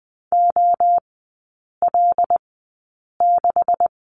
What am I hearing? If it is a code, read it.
Morse code, heard as a keyed tone, about 19 wpm, spaced as for 8 wpm, 700 Hz. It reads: OL6
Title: 127 Lesson 21e - Number 6 - Three Characters 20wpm